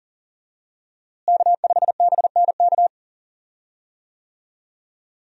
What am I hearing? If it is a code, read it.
Morse code, heard as a keyed tone, about 37 wpm, 700 Hz. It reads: K5BNK
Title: USA Call Signs - 5 Characters - Repeated 3x - 40wpm